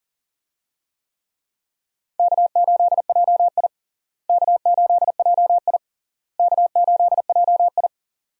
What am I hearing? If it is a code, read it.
Morse code, heard as a keyed tone, about 37 wpm, 700 Hz. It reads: K8JI K8JI K8JI
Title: USA Call Signs - 4 Characters - Repeated 3x - 40wpm